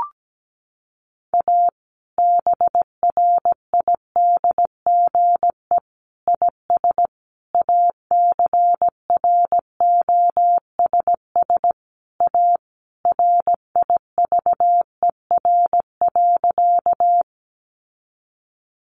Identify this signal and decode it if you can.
Morse code, heard as a keyed tone, about 17 wpm, 700 Hz. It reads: A BRIDGE IS ACROSS A RIVER.
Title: Sentences from Top 1200 Words 17wpm